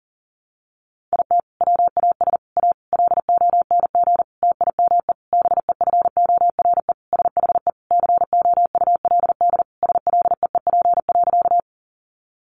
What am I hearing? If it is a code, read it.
Morse code, heard as a keyed tone, about 40 wpm, 700 Hz. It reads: IT WAS A LONG TIME BEFORE SHE COULD SLEEP.
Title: Sentences from Top 500 Words 40wpm